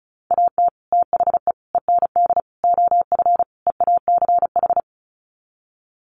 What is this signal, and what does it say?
Morse code, heard as a keyed tone, about 34 wpm, 700 Hz. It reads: AT THE END OF EACH
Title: Sets of 5 Words Encore 35wpm